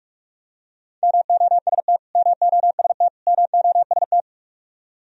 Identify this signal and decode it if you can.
Morse code, heard as a keyed tone, about 44 wpm, 700 Hz. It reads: MOST MOST MOST
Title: Top 100 Words Repeated 45wpm